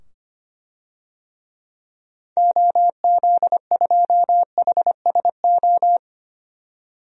Morse code, heard as a keyed tone, about 25 wpm, 700 Hz. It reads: OZ2HSO
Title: DXCC-100 Call Signs - 6 Characters - 25wpm